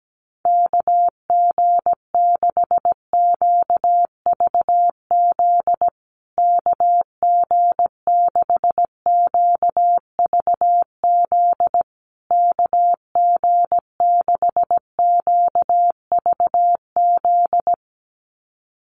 Morse code, heard as a keyed tone, about 17 wpm, 700 Hz. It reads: KG6QVZ KG6QVZ KG6QVZ
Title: USA Call Signs Repeated 3x 17wpm